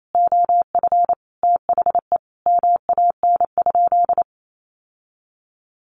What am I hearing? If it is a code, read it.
Morse code, heard as a keyed tone, about 28 wpm, 700 Hz. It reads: OF THE MAN?
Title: Sentences from Top 1500 Words 28wpm